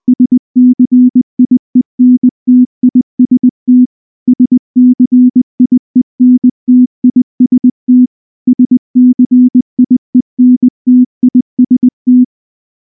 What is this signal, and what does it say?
Morse code, heard as a keyed tone, about 19 wpm, 250 Hz. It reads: SCIENTIST SCIENTIST SCIENTIST
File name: morse / morse-training-20WPM-10